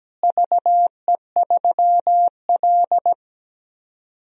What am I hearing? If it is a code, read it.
Morse code, heard as a keyed tone, about 17 wpm, 700 Hz. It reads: VE3L
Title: DXCC-100 Call Signs - 4 Characters - Repeated 3x - 17wpm